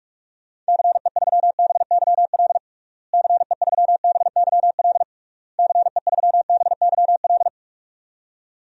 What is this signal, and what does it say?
Morse code, heard as a keyed tone, about 46 wpm, 700 Hz. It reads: CE3BYL CE3BYL CE3BYL
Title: DXCC-100 Call Signs - 6 Characters - Repeated 3x - 45wpm